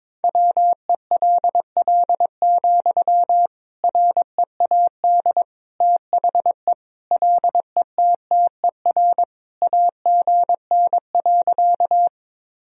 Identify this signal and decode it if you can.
Morse code, heard as a keyed tone, about 22 wpm, 700 Hz. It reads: WELL, READ THE LETTER AGN.
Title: Sentences from Top 300 Words and Abbreviations 22wpm